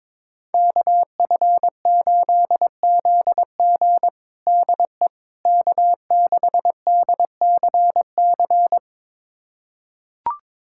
Morse code, heard as a keyed tone, about 22 wpm, 700 Hz. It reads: KF8ZG DE K6DCC
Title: USA Callsign DE Callsign - 5 Characters - 22wpm